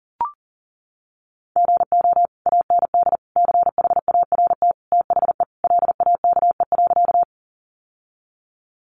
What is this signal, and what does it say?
Morse code, heard as a keyed tone, about 40 wpm, 700 Hz. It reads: GO AND CHART THE LAKE.
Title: Sentences from Top 1300 Words 40wpm